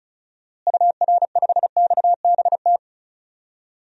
Morse code, heard as a keyed tone, about 36 wpm, 700 Hz. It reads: UR5XBT
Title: DXCC-100 Call Signs - 6 Characters - Repeated 3x - 35wpm